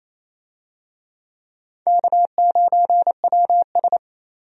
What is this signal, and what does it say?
Morse code, heard as a keyed tone, about 28 wpm, 700 Hz. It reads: K9WS
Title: USA Call Signs - 4 Characters - 28wpm